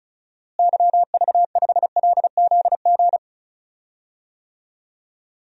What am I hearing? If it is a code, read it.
Morse code, heard as a keyed tone, about 34 wpm, 700 Hz. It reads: YV5LZG
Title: DXCC-100 Call Signs - 6 Characters - Repeated 3x - 35wpm